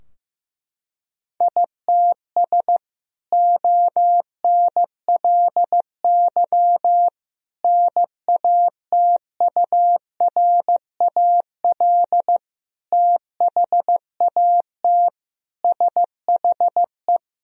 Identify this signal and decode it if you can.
Morse code, heard as a keyed tone, about 15 wpm, 700 Hz. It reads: ITS ONLY NATURAL THAT SHE
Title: Sentences from Top 1000 Words - Review Part 4 - 15wpm